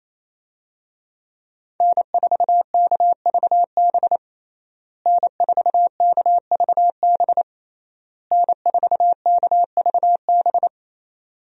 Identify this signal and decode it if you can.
Morse code, heard as a keyed tone, about 28 wpm, 700 Hz. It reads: N4KVB N4KVB N4KVB
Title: USA Call Signs - 5 Characters - Repeated 3x - 28wpm